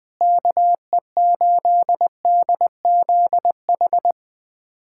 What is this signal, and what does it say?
Morse code, heard as a keyed tone, about 19 wpm, 700 Hz. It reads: KE8DZH
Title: USA Callsign DE Callsign 20wpm